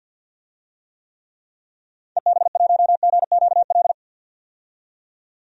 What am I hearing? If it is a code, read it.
Morse code, heard as a keyed tone, about 51 wpm, 700 Hz. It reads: EB1GQL
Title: DXCC-100 Call Signs - 6 Characters - 50wpm